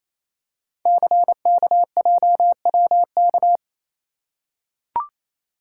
Morse code, heard as a keyed tone, about 28 wpm, 700 Hz. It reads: CKJWK
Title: Code Groups - 5 Letters - 28wpm